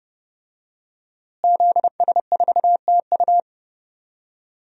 Morse code, heard as a keyed tone, about 30 wpm, 700 Hz. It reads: ZS4TU
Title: DXCC-100 Call Signs - 5 Characters - Repeated 3x - 30wpm